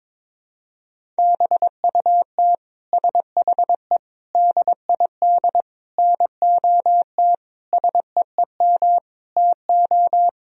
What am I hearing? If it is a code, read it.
Morse code, heard as a keyed tone, about 22 wpm, 700 Hz. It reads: BUT SHE DID NOT SEEM TO
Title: Sentences from Top 1000 Words - Review Part 1 - 22wpm